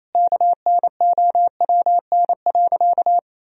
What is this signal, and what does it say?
Morse code, heard as a keyed tone, about 28 wpm, 700 Hz. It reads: KNOWN.
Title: Sentences from Top 1000 Words - Review Part 4 - 28wpm